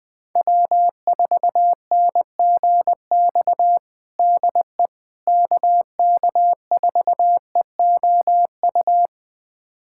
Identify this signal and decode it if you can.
Morse code, heard as a keyed tone, about 20 wpm, 700 Hz. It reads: W4NGX DE KK4EOU
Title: USA Callsign DE Callsign 20wpm